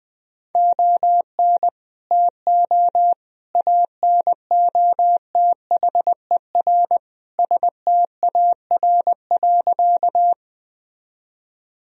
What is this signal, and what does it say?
Morse code, heard as a keyed tone, about 20 wpm, 700 Hz. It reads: ON TO ANOTHER STAR.
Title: Sentences from Top 1000 Words - Review Part 2 - 20wpm